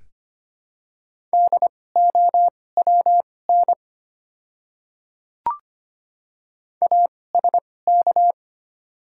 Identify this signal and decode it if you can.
Morse code, heard as a keyed tone, about 25 wpm, spaced as for 18 wpm, 700 Hz. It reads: DOWN ASK
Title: 2x Character Spacing - Top 200 Words - 25wpm